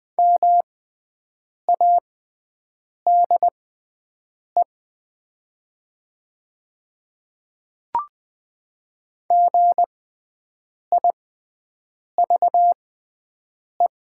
Morse code, heard as a keyed tone, about 20 wpm, spaced as for 7 wpm, 700 Hz. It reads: MADE GIVE
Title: 6x Character Spacing - Top 300 Words - 20wpm